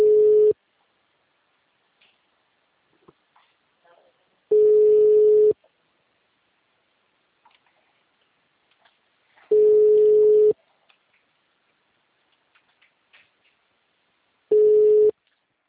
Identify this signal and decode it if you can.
Ringback tone (Europe, 425 Hz), with speech.
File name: out- -2018- - - .19379